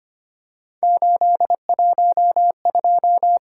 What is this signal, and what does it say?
Morse code, heard as a keyed tone, about 25 wpm, 700 Hz. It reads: 812
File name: Three Letter-Number Combinations - 25wpm